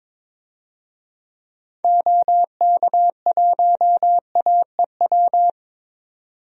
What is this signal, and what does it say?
Morse code, heard as a keyed tone, about 22 wpm, 700 Hz. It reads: OK1AEW
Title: DXCC-100 Call Signs - 6 Characters - 22wpm